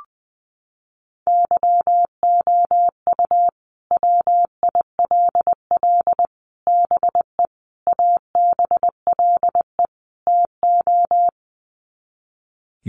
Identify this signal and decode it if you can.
Morse code, heard as a keyed tone, about 19 wpm, 700 Hz. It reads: YOU WILL BE ABLE TO
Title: Sets of 5 Words Encore2 20wpm